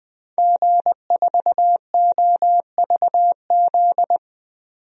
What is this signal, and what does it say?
Morse code, heard as a keyed tone, about 20 wpm, 700 Hz. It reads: G4OVZ
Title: DXCC-100 Call Signs - 5 Characters - 20wpm